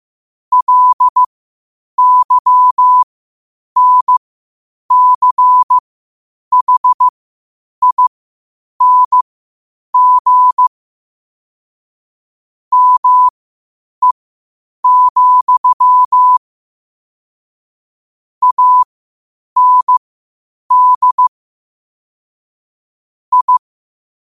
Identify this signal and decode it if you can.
Morse code, heard as a keyed tone, about 15 wpm, spaced as for 8 wpm, 1000 Hz. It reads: LYNCHING ME, AND I
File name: qotdfiles / 8wpm